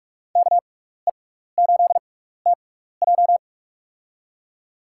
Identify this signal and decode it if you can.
Morse code, heard as a keyed tone, about 46 wpm, spaced as for 16 wpm, 700 Hz. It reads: KE8TJ
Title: USA Call Signs - 5 Characters - 6x Character Spacing - 45wpm